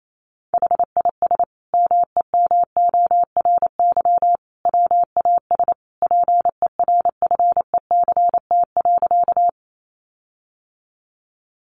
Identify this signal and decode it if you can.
Morse code, heard as a keyed tone, about 28 wpm, 700 Hz. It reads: HIS MEMORY WAS PERFECT.
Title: Sentences from Top 1300 Words 28wpm